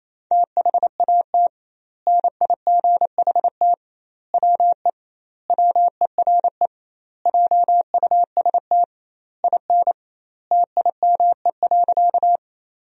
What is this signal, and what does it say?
Morse code, heard as a keyed tone, about 28 wpm, 700 Hz. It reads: THAT NIGHT WE WERE JUST IN TIME.
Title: 2x Word-Spacing - Sentences from Top 300 Words - 28wpm